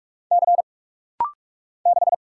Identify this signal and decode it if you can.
Morse code, heard as a keyed tone, about 43 wpm, 700 Hz. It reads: C 6
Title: Single Letter-Number Alt - Rapid-Fire 45wpm